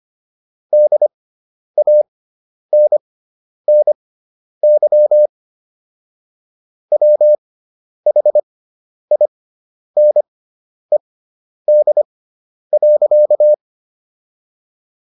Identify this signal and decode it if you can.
Morse code, heard as a keyed tone, about 25 wpm, spaced as for 10 wpm, 600 Hz. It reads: DANNY WHINED.